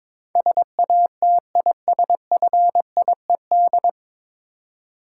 Morse code, heard as a keyed tone, about 22 wpm, 700 Hz. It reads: SATISFIED
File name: 9 Letter Words - First 3 Letters Given - 22wpm